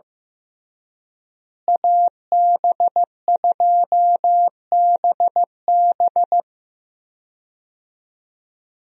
Morse code, heard as a keyed tone, about 16 wpm, 700 Hz. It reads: AB2BB
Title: USA Call Signs - 5 Characters - Repeated 3x - 15wpm